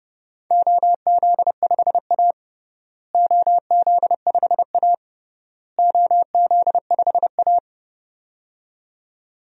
Morse code, heard as a keyed tone, about 30 wpm, 700 Hz. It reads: OZ5A OZ5A OZ5A
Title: DXCC-100 Call Signs - 4 Characters - Repeated 3x - 30wpm